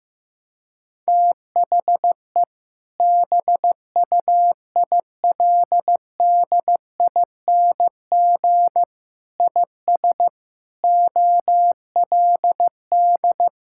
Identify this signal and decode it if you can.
Morse code, heard as a keyed tone, about 15 wpm, 700 Hz. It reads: THE BUILDING IS OLD
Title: Sentences from Top 1000 Words - Review Part 5 - 15wpm